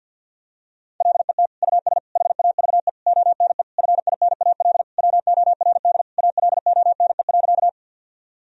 Morse code, heard as a keyed tone, about 52 wpm, 700 Hz. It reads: LET US HAVE ONE FINAL WORD ALONE.
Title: Sentences from Top 1000 Words - Review Part 2 - 50wpm